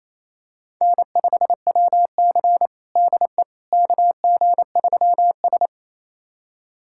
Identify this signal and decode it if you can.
Morse code, heard as a keyed tone, about 28 wpm, 700 Hz. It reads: N5WC DE KG3S
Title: USA Callsign DE Callsign - 4 Characters - 28wpm